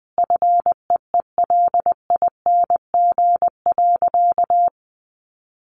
Morse code, heard as a keyed tone, about 20 wpm, 700 Hz. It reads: FEELING.
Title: Additional Easy Phrases 20wpm